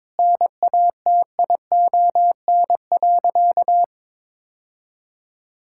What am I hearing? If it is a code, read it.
Morse code, heard as a keyed tone, about 22 wpm, 700 Hz. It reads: NATION.
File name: practice / 22wpm / Sentences with 4-Word Phrases - 22wpm